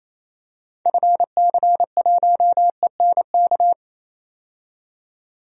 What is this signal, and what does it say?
Morse code, heard as a keyed tone, about 28 wpm, 700 Hz. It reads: FC1ENK
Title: DXCC-100 Call Signs - 6 Characters - 28wpm